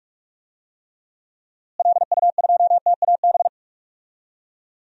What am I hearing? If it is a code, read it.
Morse code, heard as a keyed tone, about 46 wpm, 700 Hz. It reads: LU2TAB
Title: DXCC-100 Call Signs - 6 Characters - Repeated 2x - 45wpm